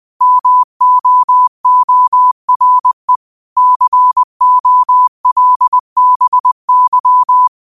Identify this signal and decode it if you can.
Morse code, heard as a keyed tone, about 20 wpm, 1000 Hz. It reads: MOORE COLBY
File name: qotdfiles / 20wpm